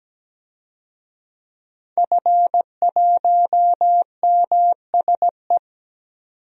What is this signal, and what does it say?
Morse code, heard as a keyed tone, about 17 wpm, 700 Hz. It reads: F1MSE
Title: DXCC-100 Call Signs - 5 Characters - 17wpm